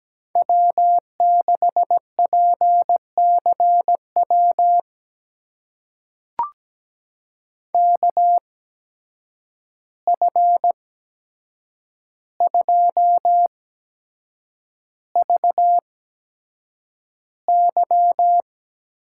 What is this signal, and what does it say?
Morse code, heard as a keyed tone, about 17 wpm, spaced as for 7 wpm, 700 Hz. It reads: W6PCW KF2VY
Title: USA Call Signs - 5 Characters - 8x Character Spacing - 17wpm